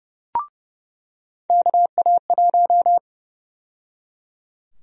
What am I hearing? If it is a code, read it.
Morse code, heard as a keyed tone, about 31 wpm, 700 Hz. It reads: KA1
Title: USA Call Sign Prefixes - 30wpm